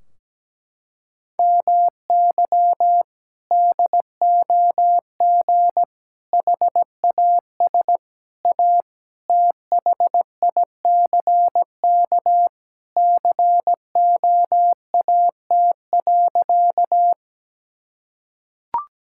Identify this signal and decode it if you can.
Morse code, heard as a keyed tone, about 17 wpm, 700 Hz. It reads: MY DOG HAS A THICK COAT.
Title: Sentences from Top 1600 Words 17wpm